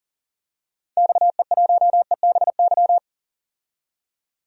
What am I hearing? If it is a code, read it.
Morse code, heard as a keyed tone, about 40 wpm, 700 Hz. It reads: XE1EBY
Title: DXCC-100 Call Signs - 6 Characters - 40wpm